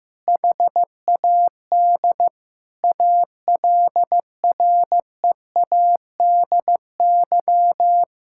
Morse code, heard as a keyed tone, about 15 wpm, 700 Hz. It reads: HAD ALREADY
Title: Sentences from Top 1600 Words 15wpm